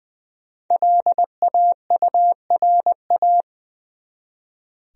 Morse code, heard as a keyed tone, about 20 wpm, 700 Hz. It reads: LAURA
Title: USA Common Names - Max 5 Letters - 20wpm